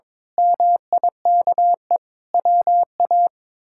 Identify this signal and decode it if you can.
Morse code, heard as a keyed tone, about 22 wpm, 700 Hz. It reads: MIKE WA
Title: K1USN SST QSO Practice - 22wpm